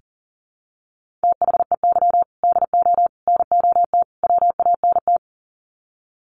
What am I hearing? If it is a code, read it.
Morse code, heard as a keyed tone, about 40 wpm, 700 Hz. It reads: THEY DO NOT WANT
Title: Sets of 4 Words 40wpm